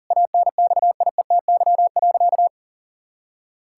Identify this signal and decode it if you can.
Morse code, heard as a keyed tone, about 40 wpm, 700 Hz. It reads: ANXIETY.
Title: Sentences from Top 1600 Words 40wpm